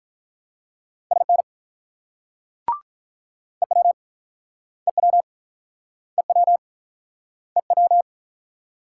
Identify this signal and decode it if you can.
Morse code, heard as a keyed tone, about 48 wpm, 700 Hz. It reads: SN EW EW EW EW